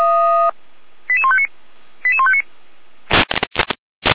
Five-tone selective call (ZVEI), heard as quick runs of five tones, tones of about 70 ms. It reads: 88159 88159